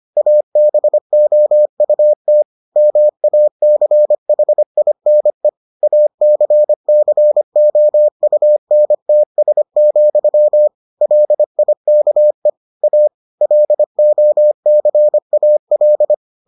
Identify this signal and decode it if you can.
Morse code, heard as a keyed tone, about 25 wpm, 600 Hz. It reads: ABOUT MACHINE ACCOUNTS, LIKE A LOCAL